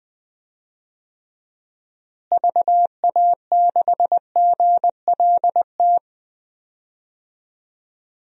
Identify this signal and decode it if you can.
Morse code, heard as a keyed tone, about 20 wpm, 700 Hz. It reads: VA6GLT
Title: DXCC-100 Call Signs - 6 Characters - Repeated 2x - 20wpm